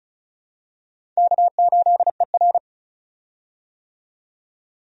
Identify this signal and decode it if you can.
Morse code, heard as a keyed tone, about 36 wpm, 700 Hz. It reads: K8ER
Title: USA Call Signs - 4 Characters - 35wpm